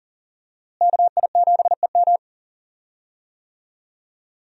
Morse code, heard as a keyed tone, about 40 wpm, 700 Hz. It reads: KI7EM
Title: USA Call Signs - 5 Characters - Repeated 2x - 40wpm